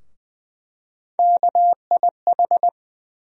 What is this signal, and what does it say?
Morse code, heard as a keyed tone, about 20 wpm, 700 Hz. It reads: KIH
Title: USA Call Sign Postfixes - 3 Letters 20wpm